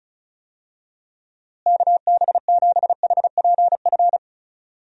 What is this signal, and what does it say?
Morse code, heard as a keyed tone, about 34 wpm, 700 Hz. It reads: KB7HPF
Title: USA Call Signs - 6 Characters - Repeated 2x - 35wpm